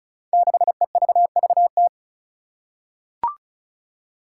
Morse code, heard as a keyed tone, about 36 wpm, 700 Hz. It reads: 6EVVT
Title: Five Challenging Letter-Number Combinations 35wpm